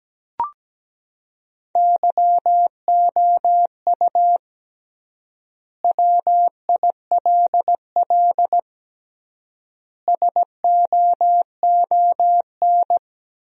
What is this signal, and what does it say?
Morse code, heard as a keyed tone, about 17 wpm, 700 Hz. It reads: YOU WILL SOON
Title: 3x Word-Spacing - Sentences from Top 400 Words - 17wpm